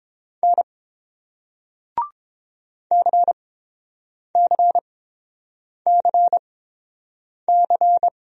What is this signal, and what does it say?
Morse code, heard as a keyed tone, about 30 wpm, spaced as for 7 wpm, 690 Hz. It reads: N CCCC